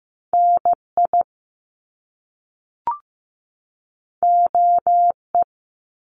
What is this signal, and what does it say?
Morse code, heard as a keyed tone, about 15 wpm, 700 Hz. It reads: NI OE
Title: Top Two Letter Combinations in Words 15wpm